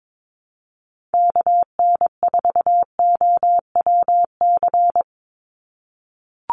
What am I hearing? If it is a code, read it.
Morse code, heard as a keyed tone, about 22 wpm, 700 Hz. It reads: KN4OWC
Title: USA Call Signs 22wpm